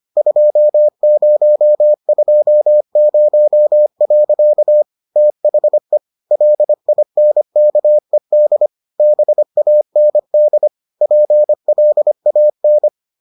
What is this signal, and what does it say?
Morse code, heard as a keyed tone, about 25 wpm, 600 Hz. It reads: 2020. THE LINKED BAND PLAN